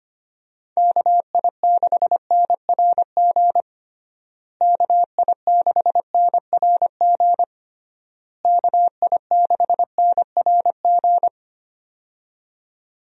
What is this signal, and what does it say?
Morse code, heard as a keyed tone, about 25 wpm, 700 Hz. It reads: KI6NRG KI6NRG KI6NRG
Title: USA Call Signs - 6 Characters - Repeated 3x - 25wpm